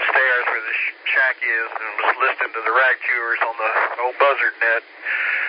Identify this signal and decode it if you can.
Emergency Alert System (SAME) tones, heard as harsh buzzing bursts, with speech. It is heard from 1.1 to 4.9 s.